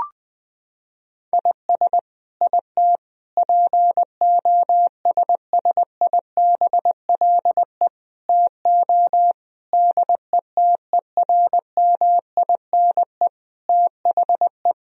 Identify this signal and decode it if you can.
Morse code, heard as a keyed tone, about 20 wpm, 700 Hz. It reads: IS IT POSSIBLE TO DETERMINE THE
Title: Sentences from Top 1000 Words - Review Part 4 - 20wpm